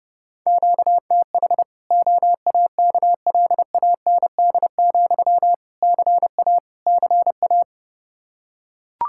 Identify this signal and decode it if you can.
Morse code, heard as a keyed tone, about 30 wpm, 700 Hz. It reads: QTH OAKLAND, CA CA
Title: Practice QSO - Exchange 1 - Beginning - 30wpm